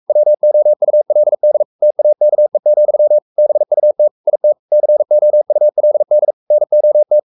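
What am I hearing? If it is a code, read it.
Morse code, heard as a keyed tone, about 44 wpm, 600 Hz. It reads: WOULD TAKE, BUT IT COULD NOT
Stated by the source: Morse code at 43 wpm sending Ken Follett's novel Never in English, from CW med Gnister.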